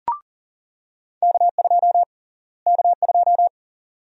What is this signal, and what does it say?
Morse code, heard as a keyed tone, about 40 wpm, 700 Hz. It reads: K2 K2
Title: USA Call Sign Prefixes - Repeated 2x - 40wpm